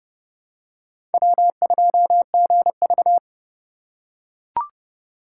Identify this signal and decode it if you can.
Morse code, heard as a keyed tone, about 28 wpm, 700 Hz. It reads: W2GV